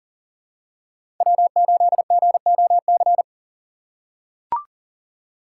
Morse code, heard as a keyed tone, about 40 wpm, 700 Hz. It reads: W8GOC
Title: USA Call Signs - 5 Characters - 4x Character Spacing - 40wpm